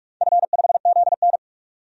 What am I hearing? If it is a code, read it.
Morse code, heard as a keyed tone, about 46 wpm, 700 Hz. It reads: F5ZN
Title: DXCC-100 Call Signs - 4 Characters - 45wpm